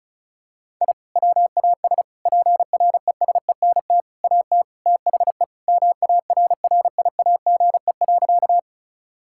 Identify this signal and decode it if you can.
Morse code, heard as a keyed tone, about 34 wpm, 700 Hz. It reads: I WAS PRESENT AT THE MARRIAGE.
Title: Sentences from Top 1000 Words - Review Part 5 - 35wpm